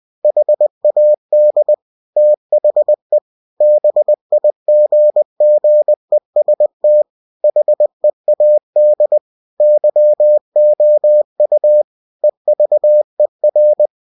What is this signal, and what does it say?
Morse code, heard as a keyed tone, about 20 wpm, 600 Hz. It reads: HAD THE BIGGEST HEAD YOU EVER
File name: cw-books / winnie-the-pooh / winnie-20wpm-page-0043